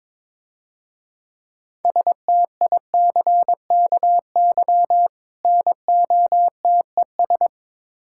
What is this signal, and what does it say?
Morse code, heard as a keyed tone, about 22 wpm, 700 Hz. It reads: STICKY NOTES
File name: Easy Words - Speed-Racing with Hints - 22wpm